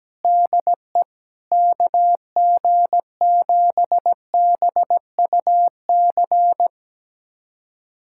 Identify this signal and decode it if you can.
Morse code, heard as a keyed tone, about 17 wpm, 700 Hz. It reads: DE KG7BUC
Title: USA Callsign DE Callsign 17wpm